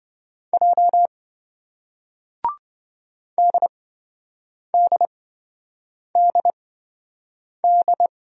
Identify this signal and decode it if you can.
Morse code, heard as a keyed tone, about 26 wpm, spaced as for 7 wpm, 700 Hz. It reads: J DDDD